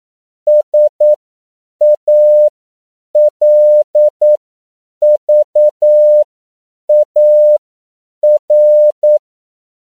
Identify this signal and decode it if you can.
Morse code, heard as a keyed tone, about 9 wpm, spaced as for 7 wpm, 600 Hz. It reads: SALVAR
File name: CW / 10PPM